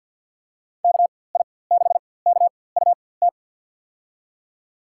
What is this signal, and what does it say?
Morse code, heard as a keyed tone, about 46 wpm, spaced as for 23 wpm, 700 Hz. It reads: KI6KUT
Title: USA Call Signs - 6 Characters - 4x Character Spacing - 50wpm